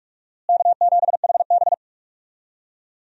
Morse code, heard as a keyed tone, about 42 wpm, 700 Hz. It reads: K7HB
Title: USA Call Signs - 4 Characters - Repeated 3x - 45wpm